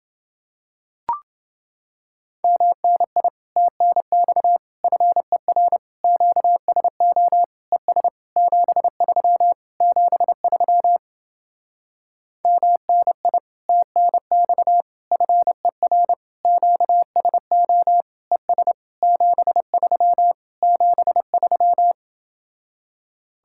Morse code, heard as a keyed tone, about 29 wpm, 700 Hz. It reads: MNI TNX FER QSO ES 73 73 MNI TNX FER QSO ES 73 73